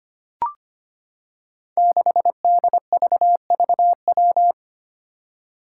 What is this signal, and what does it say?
Morse code, heard as a keyed tone, about 25 wpm, 700 Hz. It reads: 6DVVW
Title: Five Challenging Letter-Number Combinations 25wpm